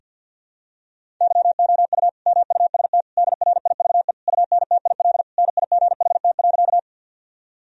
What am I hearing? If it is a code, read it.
Morse code, heard as a keyed tone, about 52 wpm, 700 Hz. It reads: YOU MUST DRIVE UNTIL NIGHT.
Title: Sentences from Top 1000 Words - Review Part 2 - 50wpm